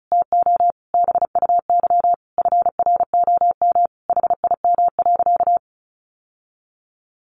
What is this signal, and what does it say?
Morse code, heard as a keyed tone, about 34 wpm, 700 Hz. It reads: TO BUY FROM HIM.
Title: Sentences from Top 1200 Words 35wpm